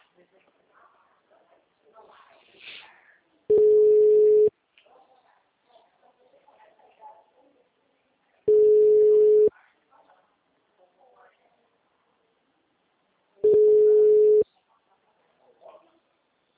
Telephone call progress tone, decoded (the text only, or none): ringback tone